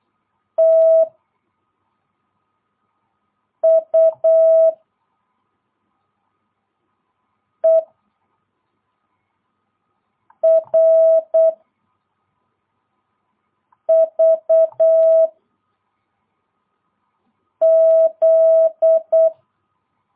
Morse code, heard as a keyed tone, about 8 wpm, 650 Hz. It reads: T U E R V Z
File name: jh8cbh / 2022cw